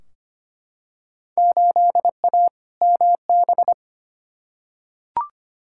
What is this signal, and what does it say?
Morse code, heard as a keyed tone, about 25 wpm, 700 Hz. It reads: 8A MB
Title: Field Day Exchanges Encore - 25wpm